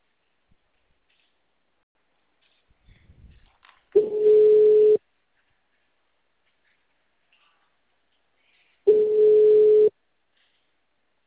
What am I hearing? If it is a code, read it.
Ringback tone (Europe, 425 Hz).